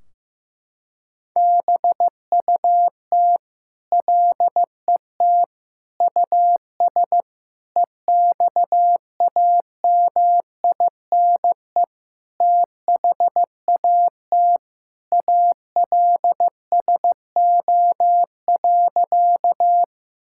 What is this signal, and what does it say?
Morse code, heard as a keyed tone, about 15 wpm, 700 Hz. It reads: BUT LET US EXAMINE THAT ALSO.
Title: Sentences from Top 1600 Words 15wpm